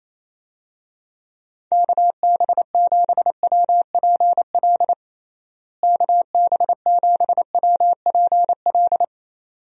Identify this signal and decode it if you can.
Morse code, heard as a keyed tone, about 27 wpm, 700 Hz. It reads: KB7WPL KB7WPL
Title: USA Call Signs - 6 Characters - Repeated 2x - 28wpm